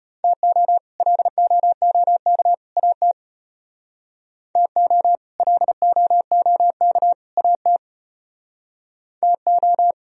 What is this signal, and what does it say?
Morse code, heard as a keyed tone, about 32 wpm, 700 Hz. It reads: TO LOOK AT TO LOOK AT TO